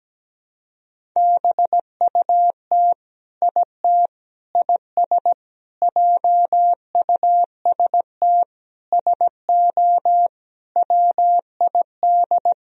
Morse code, heard as a keyed tone, about 17 wpm, 700 Hz. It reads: BUT IT IS JUST SO WID
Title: Sentences from Top 200 Words and Abbreviations 17wpm